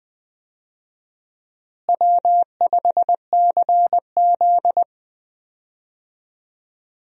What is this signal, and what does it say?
Morse code, heard as a keyed tone, about 20 wpm, 700 Hz. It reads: W5CZ
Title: USA Call Signs - 4 Characters - Repeated 2x - 20wpm